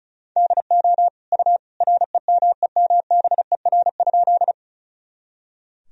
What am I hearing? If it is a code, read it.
Morse code, heard as a keyed tone, about 36 wpm, 700 Hz. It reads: DO U REMEMBER?
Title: Sentences from Top 400 Words and Abbreviations 35wpm